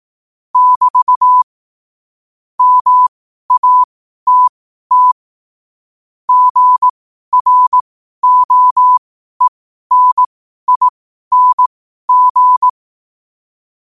Morse code, heard as a keyed tone, about 18 wpm, spaced as for 12 wpm, 1000 Hz. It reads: <BT> MATT GROENING
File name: qotdfiles / 12wpm